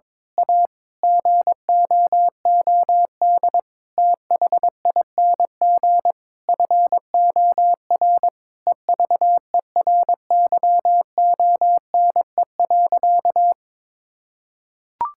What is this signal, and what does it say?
Morse code, heard as a keyed tone, about 22 wpm, 700 Hz. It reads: A GOOD THING FOR EVERYONE.
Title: Sentences from Top 1000 Words - Review Part 5 - 22wpm